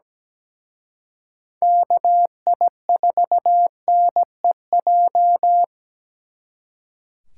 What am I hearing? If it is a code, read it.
Morse code, heard as a keyed tone, about 20 wpm, 700 Hz. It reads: KI4NEJ
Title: Speed Racing - USA Call Signs 17wpm